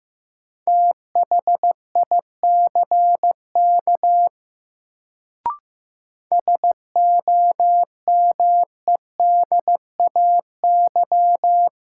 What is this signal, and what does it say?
Morse code, heard as a keyed tone, about 15 wpm, 700 Hz. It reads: THICK SOMEDAY
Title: Sentences from Top 1600 Words - New Word Drill Alt - 15wpm